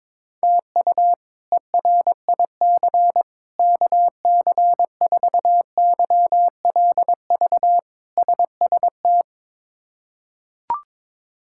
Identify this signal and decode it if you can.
Morse code, heard as a keyed tone, about 22 wpm, 700 Hz. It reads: TU ERIC KC4YLV SST
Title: K1USN SST QSO Practice - 22wpm